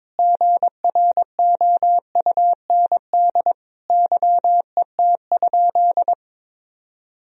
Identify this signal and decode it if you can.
Morse code, heard as a keyed tone, about 22 wpm, 700 Hz. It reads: GROUND YET?
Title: Sentences from Top 1000 Words - Review Part 3 - 22wpm